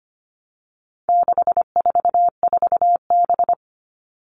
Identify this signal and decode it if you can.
Morse code, heard as a keyed tone, about 25 wpm, 700 Hz. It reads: 644B
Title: All Cut Numbers - 4-Digit 25wpm